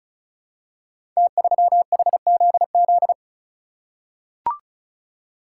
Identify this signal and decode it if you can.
Morse code, heard as a keyed tone, about 34 wpm, 700 Hz. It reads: T3HZZ
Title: Code Groups - 5 Letter:Numbers - 35wpm